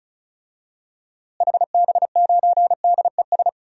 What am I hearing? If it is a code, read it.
Morse code, heard as a keyed tone, about 34 wpm, 700 Hz. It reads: HB9DES
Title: DXCC-100 Call Signs - 6 Characters - Repeated 2x - 35wpm